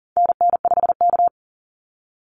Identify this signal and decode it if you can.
Morse code, heard as a keyed tone, about 40 wpm, 700 Hz. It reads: NN5K
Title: NA SOTA Activator Callsigns 40wpm